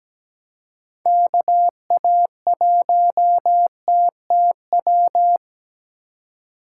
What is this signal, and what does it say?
Morse code, heard as a keyed tone, about 17 wpm, 700 Hz. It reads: KA1TTW
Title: USA Call Signs - 6 Characters - 2x Character Spacing - 17wpm